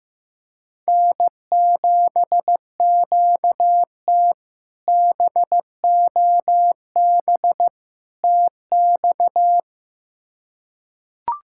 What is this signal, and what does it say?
Morse code, heard as a keyed tone, about 15 wpm, 700 Hz. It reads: N7QT BOB TX
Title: K1USN SST QSO Practice - 15wpm